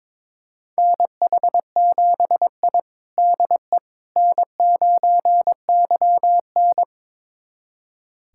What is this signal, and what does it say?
Morse code, heard as a keyed tone, about 22 wpm, 700 Hz. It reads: NH7I DE N9YN
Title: USA Callsign DE Callsign - 4 Characters - 22wpm